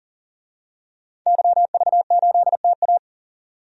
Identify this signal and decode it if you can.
Morse code, heard as a keyed tone, about 40 wpm, 700 Hz. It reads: YV8TA
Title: DXCC-100 Call Signs - 5 Characters - 40wpm